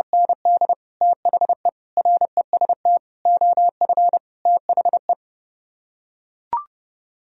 Morse code, heard as a keyed tone, about 30 wpm, 700 Hz. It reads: ND THE REST OF THE